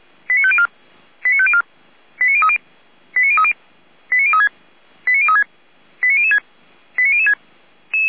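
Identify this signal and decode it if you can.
Five-tone selective call (ZVEI), heard as quick runs of five tones, tones of about 70 ms. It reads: 89504 89504 89030 89030 89036 89036 89006 89006
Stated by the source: ZVEI five-tone selective call received on an SDR, from feuerwehr-orlamuende (Feuerwehr Orlamünde fire brigade).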